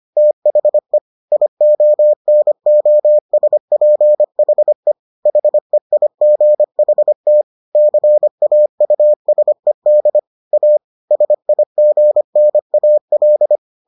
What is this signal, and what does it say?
Morse code, heard as a keyed tone, about 25 wpm, 600 Hz. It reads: THE IONOSPHE HEIGHT CAUSED A SIGNAL